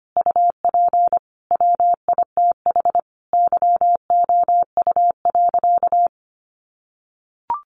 Morse code, heard as a keyed tone, about 25 wpm, 700 Hz. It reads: UP WITH YOU.
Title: Sentences from Top 300 Words 25wpm